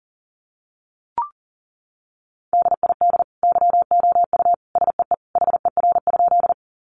Morse code, heard as a keyed tone, about 40 wpm, 700 Hz. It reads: DID YOU SEE HER?
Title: Sentences from Top 100 Words 40wpm